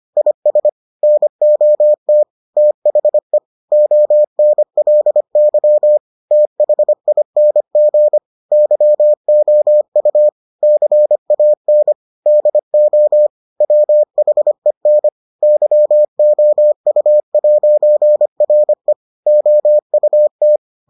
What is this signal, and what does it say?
Morse code, heard as a keyed tone, about 25 wpm, 600 Hz. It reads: IS NOT THE ONLY THING YOU CAN DO WHEN YOU'RE OUT